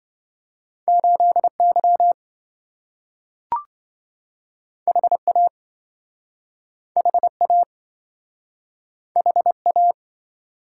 Morse code, heard as a keyed tone, about 27 wpm, 700 Hz. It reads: 8Y HA HA HA